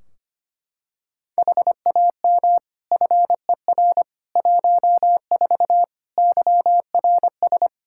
Morse code, heard as a keyed tone, about 25 wpm, 700 Hz. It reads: HAM FER 14 YRS
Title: K1USN SST QSO Practice 25wpm